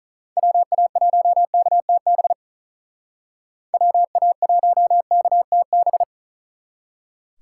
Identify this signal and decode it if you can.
Morse code, heard as a keyed tone, about 46 wpm, 700 Hz. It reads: WA1KTB WA1KTB